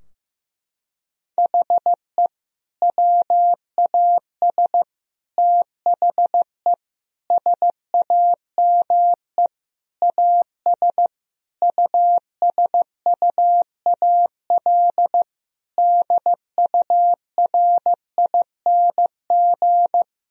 Morse code, heard as a keyed tone, about 16 wpm, 700 Hz. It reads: HE WAS THE SAME AS USUAL DURING
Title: Sentences from Top 1000 Words - Review Part 2 - 15wpm